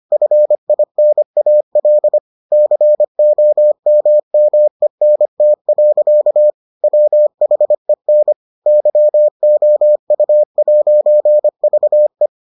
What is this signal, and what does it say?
Morse code, heard as a keyed tone, about 25 wpm, 600 Hz. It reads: FINAL COMMENT. WHEN YOU'VE